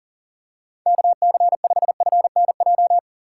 Morse code, heard as a keyed tone, about 40 wpm, 700 Hz. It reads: KC5FNJ
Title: USA Call Signs - 2x Character Spacing - 40wpm